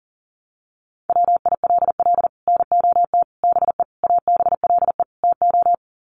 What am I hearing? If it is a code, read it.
Morse code, heard as a keyed tone, about 40 wpm, 700 Hz. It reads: WILL NOT BE ABLE TO
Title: Sets of 5 Words 40wpm